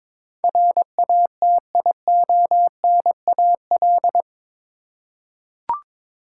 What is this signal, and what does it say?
Morse code, heard as a keyed tone, about 22 wpm, 700 Hz. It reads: RATIONAL
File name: practice / 22wpm / Eight Letter Words - 22wpm